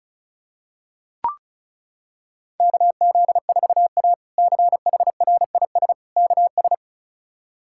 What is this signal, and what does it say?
Morse code, heard as a keyed tone, about 34 wpm, 700 Hz. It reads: KZ4A CHRIS KS
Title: K1USN SST QSO Practice - 35wpm